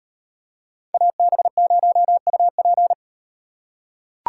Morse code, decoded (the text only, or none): AB0UP